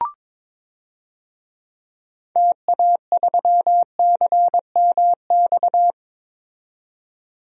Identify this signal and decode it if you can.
Morse code, heard as a keyed tone, about 22 wpm, 700 Hz. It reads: TA3CMX
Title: DXCC-100 Call Signs - 6 Characters - 22wpm